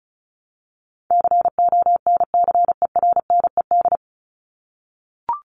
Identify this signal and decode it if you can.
Morse code, heard as a keyed tone, about 34 wpm, 700 Hz. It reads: CONCERNED
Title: Top 751-1000 Words 35wpm